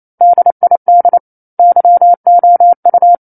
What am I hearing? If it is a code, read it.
Morse code, heard as a keyed tone, about 28 wpm, 700 Hz. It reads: DID YOU
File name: 09-Chapter-09 28 WPM